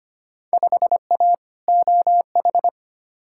Morse code, heard as a keyed tone, about 25 wpm, 700 Hz. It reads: 5A OH
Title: Field Day Exchanges Encore - 25wpm